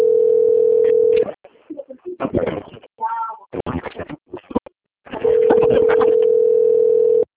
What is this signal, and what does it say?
Ringback tone (North America), 440 + 480 Hz.
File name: out- - - - - .22495